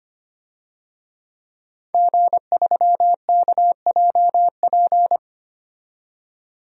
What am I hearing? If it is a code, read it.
Morse code, heard as a keyed tone, about 25 wpm, 700 Hz. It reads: G3KJP
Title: DXCC-100 Call Signs - 5 Characters - 25wpm